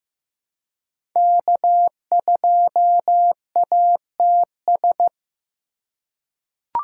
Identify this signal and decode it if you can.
Morse code, heard as a keyed tone, about 15 wpm, 700 Hz. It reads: K2ATS
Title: USA Call Signs - 3x Character Spacing - 15wpm